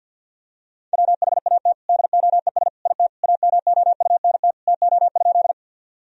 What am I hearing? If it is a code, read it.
Morse code, heard as a keyed tone, about 50 wpm, 700 Hz. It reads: WHAT DOES IT AMOUNT TO?
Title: Sentences from Top 1000 Words - Review Part 4 - 50wpm